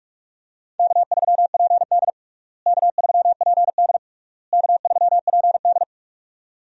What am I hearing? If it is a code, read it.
Morse code, heard as a keyed tone, about 47 wpm, 700 Hz. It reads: K3PD K3PD K3PD
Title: USA Call Signs - 4 Characters - Repeated 3x - 45wpm